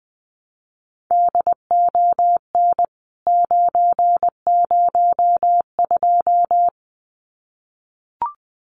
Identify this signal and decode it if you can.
Morse code, heard as a keyed tone, about 20 wpm, 700 Hz. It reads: DON 902
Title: CWT Exchange 20wpm